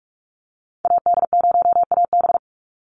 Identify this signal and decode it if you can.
Morse code, heard as a keyed tone, about 44 wpm, 700 Hz. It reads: AD0AB
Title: Call Signs Repeated 3x - 45wpm